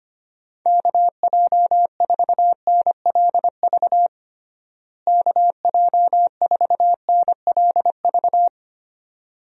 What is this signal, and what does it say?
Morse code, heard as a keyed tone, about 25 wpm, 700 Hz. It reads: KJ4NLV KJ4NLV
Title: USA Call Signs - 6 Characters - Repeated 2x - 25wpm